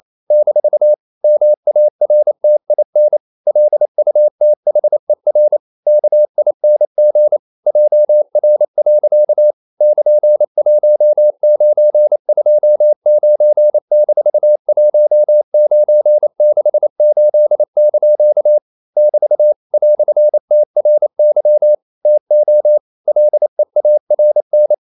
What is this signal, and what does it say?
Morse code, heard as a keyed tone, about 28 wpm, 600 Hz. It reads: - MARTIN LUTHER KING JR. <KN>1929-1968) <BT> "TRY TO LEARN
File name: CW / Practice / EnglishQuotations0128